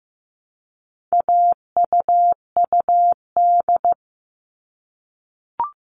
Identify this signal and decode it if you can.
Morse code, heard as a keyed tone, about 15 wpm, 700 Hz. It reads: AUUD
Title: All Cut Numbers - 4-Digit 15wpm